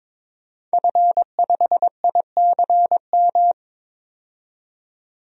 Morse code, heard as a keyed tone, about 21 wpm, 700 Hz. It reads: F5ICM
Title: DXCC-100 Call Signs - 5 Characters - 22wpm